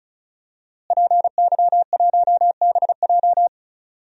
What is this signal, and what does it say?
Morse code, heard as a keyed tone, about 36 wpm, 700 Hz. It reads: PY1BJ
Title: DXCC-100 Call Signs - 5 Characters - Repeated 3x - 35wpm